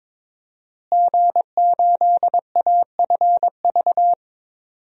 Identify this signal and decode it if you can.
Morse code, heard as a keyed tone, about 22 wpm, 700 Hz. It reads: G8AFV
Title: DXCC-100 Call Signs - 5 Characters - Repeated 2x - 22wpm